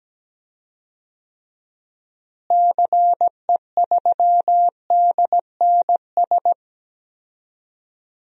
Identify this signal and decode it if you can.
Morse code, heard as a keyed tone, about 17 wpm, 700 Hz. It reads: CE3DNS
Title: DXCC-100 Call Signs - 6 Characters - Repeated 2x - 17wpm